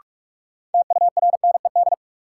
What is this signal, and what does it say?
Morse code, heard as a keyed tone, about 46 wpm, 700 Hz. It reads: TURNED
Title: Sets of 4 Words - New Word Drill - 45wpm